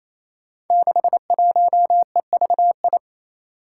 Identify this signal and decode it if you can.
Morse code, heard as a keyed tone, about 29 wpm, 700 Hz. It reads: 61EVI
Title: Five Challenging Letter-Number Combinations 28wpm